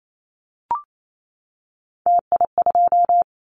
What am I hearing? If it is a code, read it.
Morse code, heard as a keyed tone, about 28 wpm, 700 Hz. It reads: TI2
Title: Callsign Prefixes for Top 100 DXCC - 28wpm